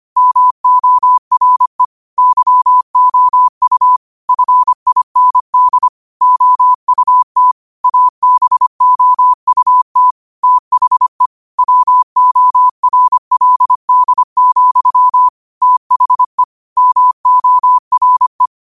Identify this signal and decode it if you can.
Morse code, heard as a keyed tone, about 25 wpm, 1000 Hz. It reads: MORE YOU FIND OUT ABOUT THE WORLD, THE MORE
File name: qotdfiles / 25wpm